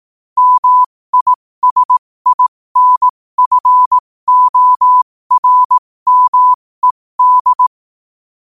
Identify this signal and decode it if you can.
Morse code, heard as a keyed tone, about 18 wpm, spaced as for 15 wpm, 1000 Hz. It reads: MISINFORMED
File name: qotdfiles / 15wpm